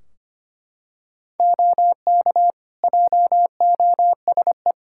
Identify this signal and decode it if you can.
Morse code, heard as a keyed tone, about 25 wpm, 700 Hz. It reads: OK JOSE
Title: Practice QSO - Exchange 3 - More Info - 25wpm